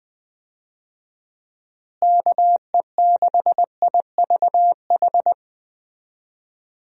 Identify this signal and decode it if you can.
Morse code, heard as a keyed tone, about 20 wpm, 700 Hz. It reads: KE6IVH
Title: USA Call Signs - 6 Characters - Repeated 2x - 20wpm